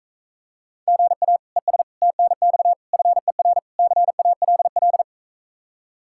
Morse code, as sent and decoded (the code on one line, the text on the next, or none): --. .- / . ... / - -. -..- / ..-. .-. / -.-. .- .-.. .-..
GA ES TNX FER CALL